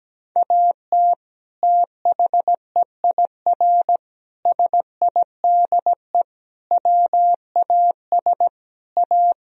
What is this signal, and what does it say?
Morse code, heard as a keyed tone, about 17 wpm, 700 Hz. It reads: AT THEIR SIDE WAS A
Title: Sentences from Top 1000 Words - Review Part 2 - 17wpm